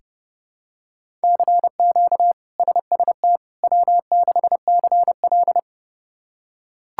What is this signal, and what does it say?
Morse code, heard as a keyed tone, about 30 wpm, 700 Hz. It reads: CQ SST W6CL
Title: K1USN SST QSO Practice - 30wpm